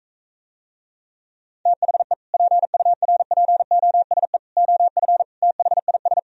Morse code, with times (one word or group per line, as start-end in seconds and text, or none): 1.7-2.1 THE
2.3-4.4 PURPOSE
4.6-5.2 OF
5.4-6.2 THIS